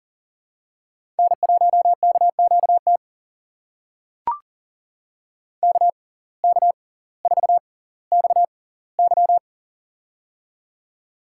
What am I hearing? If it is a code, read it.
Morse code, heard as a keyed tone, about 40 wpm, spaced as for 20 wpm, 690 Hz. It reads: N1KQT KK4XY
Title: USA Call Signs - 5 Characters - 6x Character Spacing - 40wpm